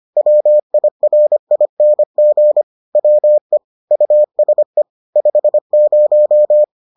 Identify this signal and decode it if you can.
Morse code, heard as a keyed tone, about 25 wpm, 600 Hz. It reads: WIRING WE USE 50